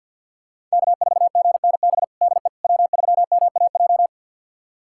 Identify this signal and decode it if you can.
Morse code, heard as a keyed tone, about 48 wpm, 700 Hz. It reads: K4GNB DE W3MAJ
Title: USA Callsign DE Callsign - 5 Characters - 50wpm